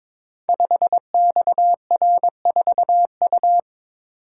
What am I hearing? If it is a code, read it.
Morse code, heard as a keyed tone, about 22 wpm, 700 Hz. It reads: 5XR4U
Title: Code Groups - 5 Letter:Numbers - 22wpm